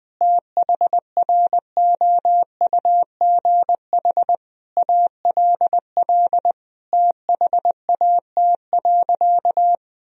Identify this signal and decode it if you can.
Morse code, heard as a keyed tone, about 21 wpm, 700 Hz. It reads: THROUGH ALL THAT.
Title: Sentences from Top 1000 Words - Review Part 1 - 20wpm